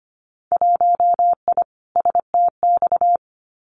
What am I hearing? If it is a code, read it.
Morse code, heard as a keyed tone, about 25 wpm, 700 Hz. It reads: 1I STX
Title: Winter Field Day Exchanges 25wpm